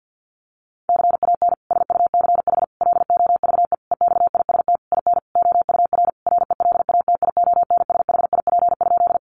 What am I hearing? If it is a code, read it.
Morse code, heard as a keyed tone, about 50 wpm, 700 Hz. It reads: CAN SUCH LOVE EXIST IN OUR RELATIONSHIP?
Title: Sentences from Top 1100 Words 50wpm